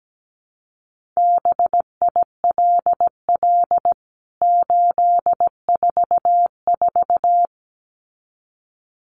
Morse code, heard as a keyed tone, about 17 wpm, 700 Hz. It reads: BILL 844
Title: CWT Exchange 17wpm